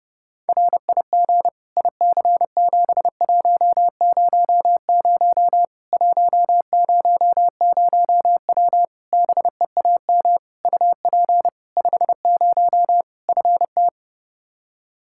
Morse code, heard as a keyed tone, about 30 wpm, 700 Hz. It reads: RIG IC7100 100W BEAM UP 50 FT
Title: Practice QSO - Exchange 2 - Info - 30wpm